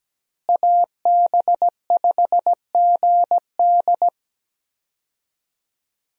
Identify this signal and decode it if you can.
Morse code, heard as a keyed tone, about 17 wpm, 700 Hz. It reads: AB5GD
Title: USA Call Signs - 5 Characters - Repeated 2x - 17wpm